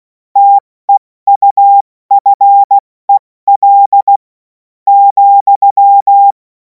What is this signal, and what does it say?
Morse code, heard as a keyed tone, about 16 wpm, spaced as for 13 wpm, 800 Hz. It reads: TEUFEL,
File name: Morsekurs / Witze / Witz 03. März 2026 70 F80